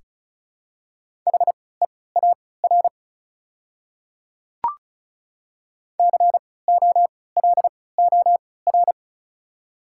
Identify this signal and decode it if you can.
Morse code, heard as a keyed tone, about 36 wpm, spaced as for 20 wpm, 690 Hz. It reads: HEAR COLOR
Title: 3x Character Spacing - Top 300 Words - 35wpm